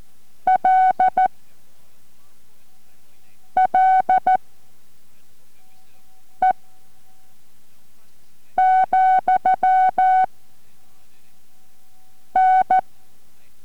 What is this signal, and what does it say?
Morse code, heard as a keyed tone, about 14 wpm, spaced as for 4 wpm, 750 Hz. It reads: LLE, N